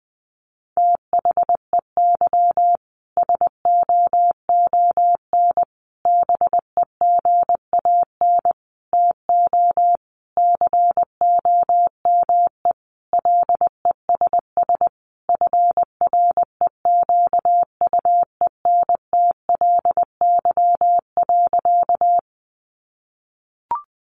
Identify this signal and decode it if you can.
Morse code, heard as a keyed tone, about 20 wpm, 700 Hz. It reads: THEY SOON BEGAN TO COME LESS FREQUENTLY.
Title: Sentences from Top 1300 Words 20wpm